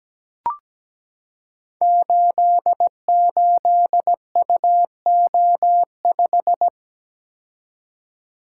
Morse code, heard as a keyed tone, about 17 wpm, 700 Hz. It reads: 88UO5
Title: Code Groups - 5 Letter:Numbers - 17wpm